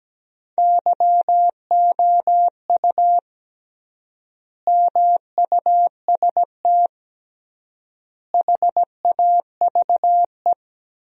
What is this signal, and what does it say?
Morse code, heard as a keyed tone, about 17 wpm, 700 Hz. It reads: YOU MUST HAVE
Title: 3x Word-Spacing - Sentences from Top 500 Words - 17wpm